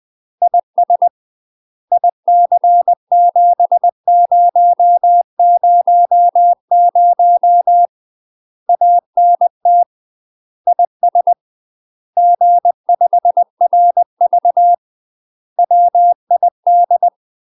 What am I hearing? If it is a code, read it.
Morse code, heard as a keyed tone, about 20 wpm, 700 Hz. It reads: IS IC7000 ANT IS G5RV WID
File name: UK MASTER QSOs TRAINING / 20WPM / UK MASTER QSOs 0071